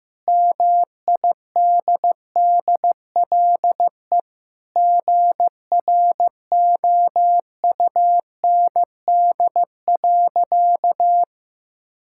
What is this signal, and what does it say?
Morse code, heard as a keyed tone, about 15 wpm, 700 Hz. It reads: MIDDLE GROUND.
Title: Sentences from Top 1600 Words 15wpm